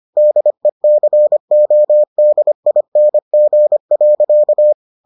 Morse code, heard as a keyed tone, about 25 wpm, 600 Hz. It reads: DECODING.